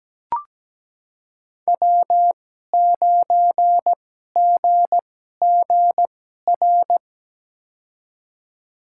Morse code, heard as a keyed tone, about 17 wpm, spaced as for 12 wpm, 700 Hz. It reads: W9GGR
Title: USA Call Signs - 2x Character Spacing - 17wpm